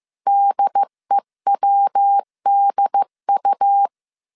Morse code, heard as a keyed tone, about 15 wpm, 800 Hz. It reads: DEWDU
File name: CW / DET / 1 3 1500C S A 14wpm 2447